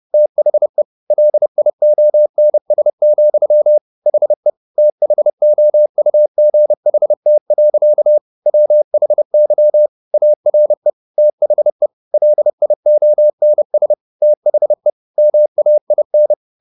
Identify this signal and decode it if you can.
Morse code, heard as a keyed tone, about 30 wpm, 600 Hz. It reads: THE LIONS, HE THOUGHT. WHY ARE THE LIONS THE MAIN
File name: the old man and the sea / the old man and the sea 1 30wpm